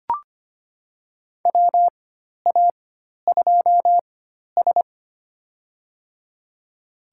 Morse code, heard as a keyed tone, about 25 wpm, spaced as for 12 wpm, 700 Hz. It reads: WA2S
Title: USA Call Signs - 4 Characters - 4x Character Spacing - 25wpm